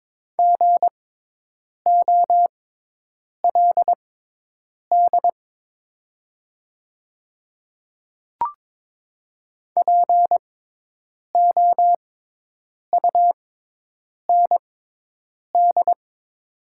Morse code, heard as a keyed tone, about 21 wpm, spaced as for 8 wpm, 700 Hz. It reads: GOLD POUND
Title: 6x Character Spacing - Top 500 Words - 22wpm